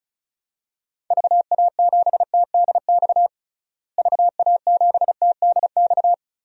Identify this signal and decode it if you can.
Morse code, heard as a keyed tone, about 36 wpm, 700 Hz. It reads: VA7TDX VA7TDX
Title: DXCC-100 Call Signs - 6 Characters - Repeated 2x - 35wpm